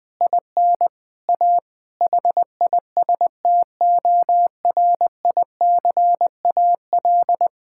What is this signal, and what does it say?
Morse code, heard as a keyed tone, about 20 wpm, 700 Hz. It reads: IN A HISTORICAL
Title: Sentences from Top 1000 Words - Review Part 5 - 20wpm